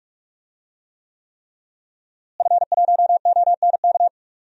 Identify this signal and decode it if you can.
Morse code, heard as a keyed tone, about 46 wpm, 700 Hz. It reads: F1ONK
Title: DXCC-100 Call Signs - 5 Characters - Repeated 3x - 45wpm